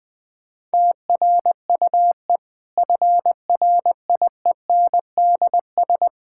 Morse code, heard as a keyed tone, about 20 wpm, 700 Hz. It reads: TRUE FRIENDS
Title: Difficult Sentences with Hints - 20wpm